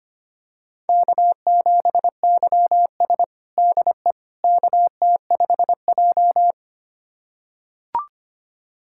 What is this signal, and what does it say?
Morse code, heard as a keyed tone, about 25 wpm, 700 Hz. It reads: K7YS DE KT5J
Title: USA Callsign DE Callsign - 4 Characters - 25wpm